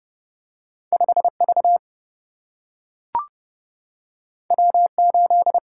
Morse code, heard as a keyed tone, about 30 wpm, 700 Hz. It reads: 5 V W 8
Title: Single-Letter Single-Number 30wpm